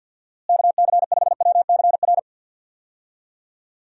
Morse code, heard as a keyed tone, about 47 wpm, 700 Hz. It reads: KC5WCR
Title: USA Call Signs - 6 Characters - Repeated 2x - 50wpm